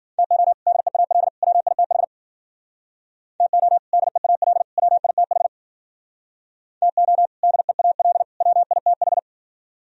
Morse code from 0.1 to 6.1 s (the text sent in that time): TO DEAL WITH TO DEAL WITH